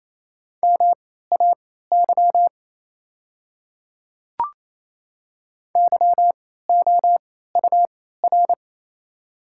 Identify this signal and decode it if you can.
Morse code, heard as a keyed tone, about 28 wpm, spaced as for 16 wpm, 700 Hz. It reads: MAY YOUR
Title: 3x Character Spacing - Top 500 Words - 28wpm